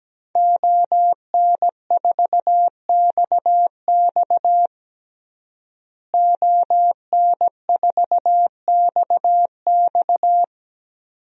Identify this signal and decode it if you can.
Morse code, heard as a keyed tone, about 17 wpm, 700 Hz. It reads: ON4XX ON4XX
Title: DXCC-100 Call Signs - 5 Characters - Repeated 2x - 17wpm